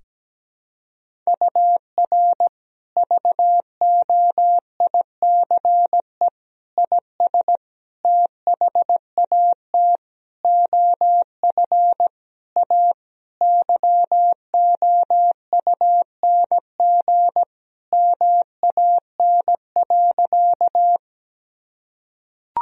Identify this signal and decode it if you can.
Morse code, heard as a keyed tone, about 17 wpm, 700 Hz. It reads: UR VOICE IS THAT OF A YOUNG MAN.
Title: Sentences from Top 500 Words and Abbreviations 17wpm